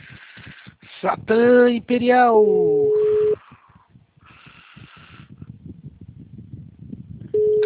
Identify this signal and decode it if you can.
Ringback tone (Europe, 425 Hz), with speech.